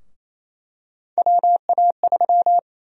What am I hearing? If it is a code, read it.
Morse code, heard as a keyed tone, about 28 wpm, 700 Hz. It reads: WA3
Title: USA Call Sign Prefixes - Repeated 2x - 28wpm